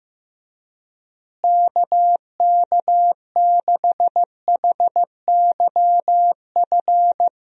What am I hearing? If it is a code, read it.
Morse code, heard as a keyed tone, about 15 wpm, 700 Hz. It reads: KK6HYF
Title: USA Call Signs - 6 Characters - Repeated 3x - 15wpm